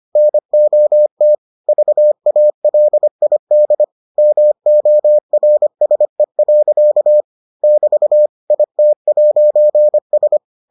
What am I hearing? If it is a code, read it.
Morse code, heard as a keyed tone, about 25 wpm, 600 Hz. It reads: NOT VALID MORSE. <BT> IT'S